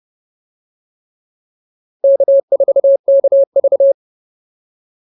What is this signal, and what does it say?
Morse code, heard as a keyed tone, about 30 wpm, 550 Hz. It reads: K4KV